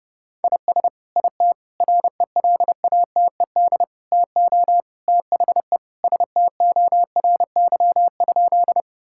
Morse code, heard as a keyed tone, about 30 wpm, 700 Hz. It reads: IS IT RELATED TO THE STORY?
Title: Sentences from Top 1000 Words - Review Part 5 - 30wpm